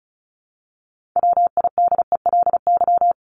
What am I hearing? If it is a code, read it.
Morse code, heard as a keyed tone, about 34 wpm, 700 Hz. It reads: WIDELY
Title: Top 1001-1250 Words 35wpm